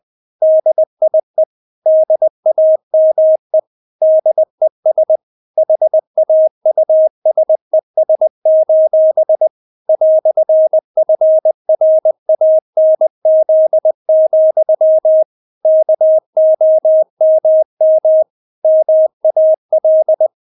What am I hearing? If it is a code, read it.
Morse code, heard as a keyed tone, about 20 wpm, 630 Hz. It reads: DIE DAME DES HAUSES: "FRANZ, KOMM MAL
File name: Witz1 20wpm